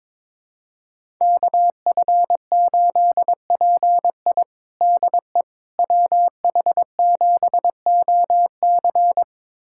Morse code, heard as a keyed tone, about 22 wpm, 700 Hz. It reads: KF8PI DE WH7OC
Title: USA Callsign DE Callsign - 5 Characters - 22wpm